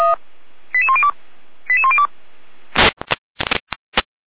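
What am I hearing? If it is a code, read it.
Five-tone selective call (ZVEI), heard as quick runs of five tones, tones of about 70 ms. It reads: 88102 88102
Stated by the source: ZVEI five-tone selective call received with an SDR.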